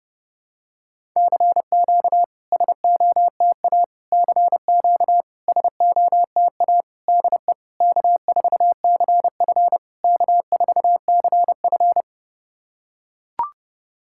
Morse code, heard as a keyed tone, about 30 wpm, 690 Hz. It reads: CQ SOTA CQ SOTA DE K4CF K4CF
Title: SOTA QSO Practice 30wpm